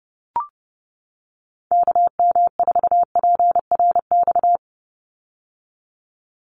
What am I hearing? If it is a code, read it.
Morse code, heard as a keyed tone, about 30 wpm, 700 Hz. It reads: KM4PRX
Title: USA Call Signs 30wpm